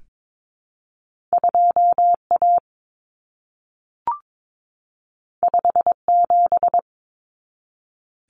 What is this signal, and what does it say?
Morse code, heard as a keyed tone, about 22 wpm, 700 Hz. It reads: 2A 57
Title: Common Cut Numbers - 2-Digit 22wpm